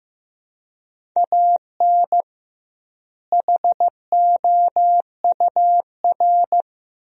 Morse code, heard as a keyed tone, about 15 wpm, 700 Hz. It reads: AN HOUR
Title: 2x Word-Spacing - Sentences from Top 400 Words - 15wpm